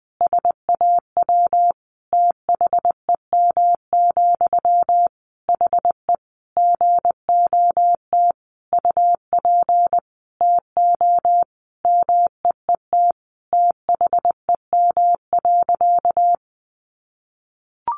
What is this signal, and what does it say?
Morse code, heard as a keyed tone, about 20 wpm, 700 Hz. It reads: SAW THEM, HE GOT UP TO MEET THEM.
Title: Sentences from Top 1100 Words 20wpm